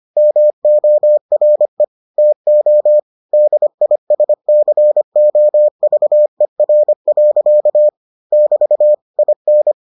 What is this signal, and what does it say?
Morse code, heard as a keyed tone, about 25 wpm, 600 Hz. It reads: MORE TO DISCOVER. <BT> IN